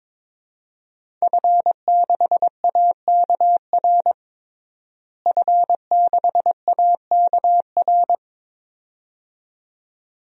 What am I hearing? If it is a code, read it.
Morse code, heard as a keyed tone, about 22 wpm, 700 Hz. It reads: F6AKR F6AKR
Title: DXCC-100 Call Signs - 5 Characters - Repeated 2x - 22wpm